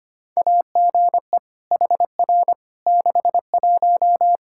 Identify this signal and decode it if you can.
Morse code, heard as a keyed tone, about 25 wpm, 700 Hz. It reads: AGE HR 61
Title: Practice QSO - Exchange 3 - More Info - 25wpm